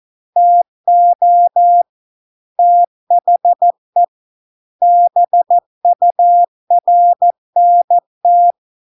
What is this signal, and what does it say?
Morse code, heard as a keyed tone, about 14 wpm, 700 Hz. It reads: TO THE BURNT